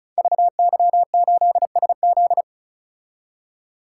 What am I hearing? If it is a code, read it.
Morse code, heard as a keyed tone, about 36 wpm, 700 Hz. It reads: VY8SZ
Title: Code Groups - 5 Letter:Numbers - 35wpm